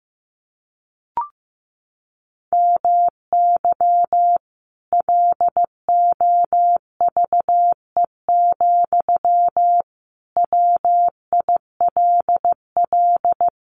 Morse code, heard as a keyed tone, about 15 wpm, 700 Hz. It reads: MY LOVE, WILL
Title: Sentences from Top 1200 Words 15wpm